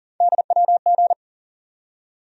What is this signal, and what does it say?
Morse code, heard as a keyed tone, about 40 wpm, 700 Hz. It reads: DWG
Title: USA Call Sign Postfixes - 3 Letters 40wpm